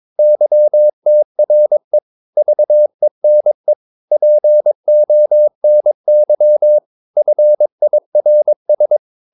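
Morse code, heard as a keyed tone, about 22 wpm, 600 Hz. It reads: YTRE VENE PONY FIRS
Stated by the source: Morse practice of short Danish words at 22 wpm, from CW med Gnister.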